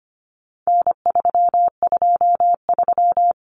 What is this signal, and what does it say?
Morse code, heard as a keyed tone, about 25 wpm, 700 Hz. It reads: N323
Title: Common Cut Numbers - 4-Digit 25wpm